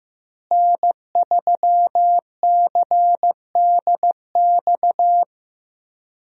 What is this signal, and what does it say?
Morse code, heard as a keyed tone, about 15 wpm, 700 Hz. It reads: N3CDX
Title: USA Call Signs - 5 Characters - Repeated 2x - 15wpm